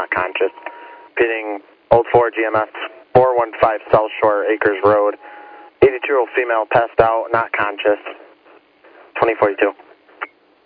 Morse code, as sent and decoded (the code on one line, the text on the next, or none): none